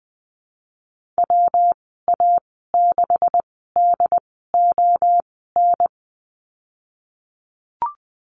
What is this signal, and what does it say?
Morse code, heard as a keyed tone, about 20 wpm, spaced as for 15 wpm, 700 Hz. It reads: WA6DON
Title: 147 Lesson 24g - Letter y - Call Signs 20wpm